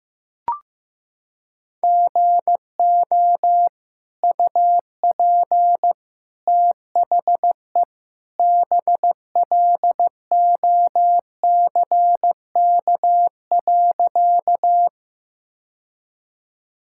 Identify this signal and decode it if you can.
Morse code, heard as a keyed tone, about 15 wpm, 700 Hz. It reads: GO UP THE BLOCK.
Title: Sentences from Top 1600 Words 15wpm